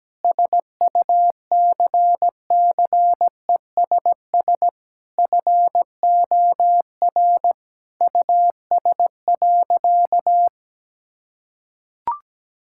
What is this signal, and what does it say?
Morse code, heard as a keyed tone, about 17 wpm, 700 Hz. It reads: SUCCESS FOR US.
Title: Sentences from Top 1600 Words 17wpm